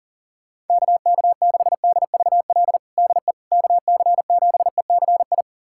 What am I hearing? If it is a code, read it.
Morse code, heard as a keyed tone, about 40 wpm, 700 Hz. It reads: KK6DVL DE KC7ECI
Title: USA Callsign DE Callsign - 6 Characters - 40wpm